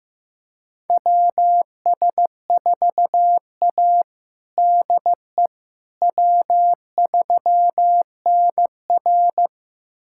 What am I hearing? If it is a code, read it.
Morse code, heard as a keyed tone, about 15 wpm, 700 Hz. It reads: WS4A DE W3NR
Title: USA Callsign DE Callsign - 4 Characters - 15wpm